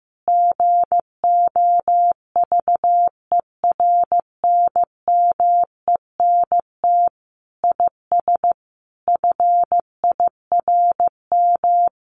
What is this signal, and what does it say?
Morse code, heard as a keyed tone, about 15 wpm, 700 Hz. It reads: GOVERNMENT IS FIRM
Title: Sentences from Top 1500 Words 15wpm